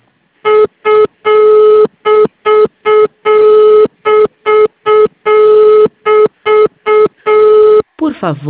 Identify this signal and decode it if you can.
Congestion tone (425 Hz), with speech.